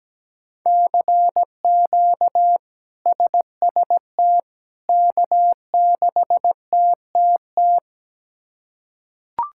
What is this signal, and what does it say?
Morse code, heard as a keyed tone, about 17 wpm, 700 Hz. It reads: CQ SST K6TTT
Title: K1USN SST QSO Practice - 17wpm